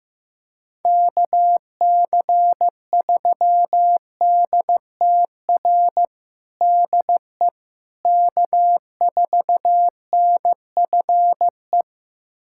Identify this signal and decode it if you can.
Morse code, heard as a keyed tone, about 15 wpm, 700 Hz. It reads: KC3DTR DE K4NFE
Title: USA Callsign DE Callsign 15wpm